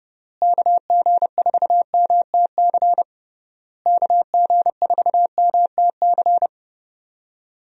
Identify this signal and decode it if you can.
Morse code, heard as a keyed tone, about 30 wpm, 700 Hz. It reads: KG4MTC KG4MTC
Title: USA Call Signs - 6 Characters - Repeated 3x - 30wpm